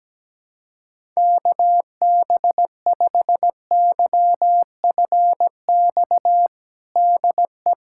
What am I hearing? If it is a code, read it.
Morse code, heard as a keyed tone, about 17 wpm, 700 Hz. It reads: KB5YFX DE
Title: USA Callsign DE Callsign - 6 Characters - 17wpm